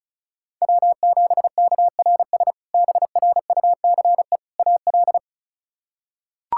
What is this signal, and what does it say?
Morse code, heard as a keyed tone, about 36 wpm, 700 Hz. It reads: W7KRS BRUCE AL
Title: K1USN SST QSO Practice - 35wpm